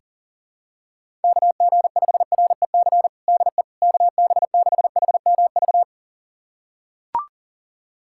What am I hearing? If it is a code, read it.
Morse code, heard as a keyed tone, about 40 wpm, 700 Hz. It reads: KG5REC DE KB6HMV
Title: USA Callsign DE Callsign - 6 Characters - 40wpm